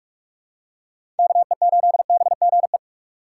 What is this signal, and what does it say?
Morse code, heard as a keyed tone, about 41 wpm, 700 Hz. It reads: KE8BGE
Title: USA Call Signs - 6 Characters - 4x Character Spacing - 45wpm